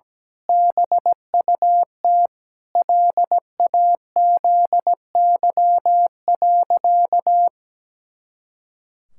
Morse code, heard as a keyed tone, about 17 wpm, 700 Hz. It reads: BUT LAZY.
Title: Pangram Challenge v2 - 17wpm